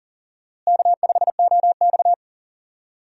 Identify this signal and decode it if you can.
Morse code, heard as a keyed tone, about 40 wpm, 700 Hz. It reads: K5OX
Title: USA Call Signs - 4 Characters - Repeated 3x - 40wpm